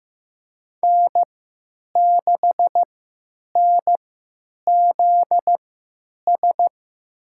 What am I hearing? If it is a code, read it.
Morse code, heard as a keyed tone, about 15 wpm, spaced as for 9 wpm, 700 Hz. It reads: N6NZS
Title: USA Call Signs - 5 Characters - 3x Character Spacing - 15wpm